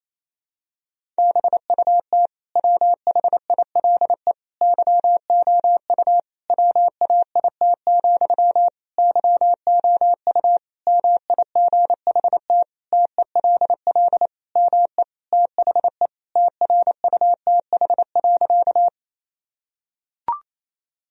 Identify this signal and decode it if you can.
Morse code, heard as a keyed tone, about 28 wpm, 700 Hz. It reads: BUT WHILE YOU WAIT, YOU MIGHT TELL ME THE TRUTH.
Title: Sentences from Top 1000 Words - Review Part 1 - 28wpm